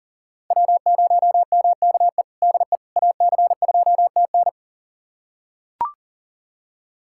Morse code, heard as a keyed tone, about 38 wpm, 700 Hz. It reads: W0MKE DE AC2TN
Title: USA Callsign DE Callsign - 5 Characters - 40wpm